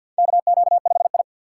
Morse code, heard as a keyed tone, about 51 wpm, 700 Hz. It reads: KQ5I
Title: USA Call Signs - 4 Characters - 2x Character Spacing - 50wpm